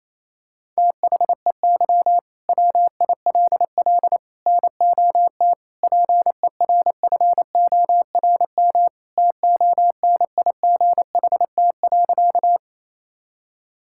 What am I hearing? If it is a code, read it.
Morse code, heard as a keyed tone, about 28 wpm, 700 Hz. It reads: THEY WILL NOT PERFORM TONIGHT.
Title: Sentences from Top 1000 Words - Review Part 5 - 28wpm